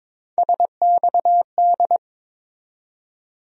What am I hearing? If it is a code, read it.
Morse code, heard as a keyed tone, about 22 wpm, 700 Hz. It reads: SXD
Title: USA Call Sign Postfixes - 3 Letters 22wpm